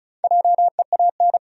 Morse code, heard as a keyed tone, about 35 wpm, 700 Hz. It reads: JEAN
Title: USA Common Names - Max 4 Letters - 35wpm